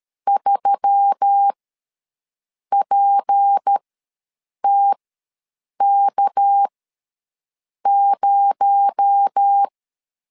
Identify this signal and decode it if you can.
Morse code, heard as a keyed tone, about 13 wpm, spaced as for 6 wpm, 800 Hz. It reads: 3PTK0
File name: LEZIONI / 8 YJ45 / 8 Y ws 12wpm 3409